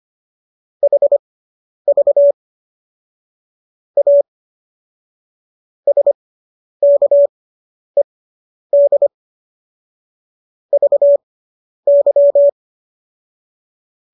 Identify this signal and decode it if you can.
Morse code, heard as a keyed tone, about 25 wpm, spaced as for 10 wpm, 570 Hz. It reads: HV A SKED VY